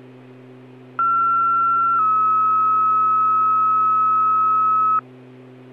Two-tone page, heard as a one-second tone, then a three-second tone, about 1360 then 1290 Hz. It is heard from 1.0 to 5.0 s.